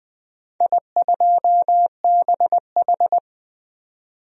Morse code, heard as a keyed tone, about 20 wpm, 700 Hz. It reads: I2BH
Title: DXCC-100 Call Signs - 4 Characters - 20wpm